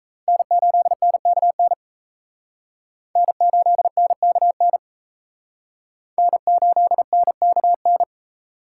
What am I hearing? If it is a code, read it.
Morse code, heard as a keyed tone, about 35 wpm, 700 Hz. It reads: N8NKN N8NKN N8NKN